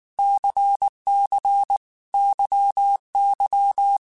Morse code, heard as a keyed tone, about 19 wpm, 800 Hz. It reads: CC YY
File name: Groups@18 / Groups@18 mp3 / LFCYQ@18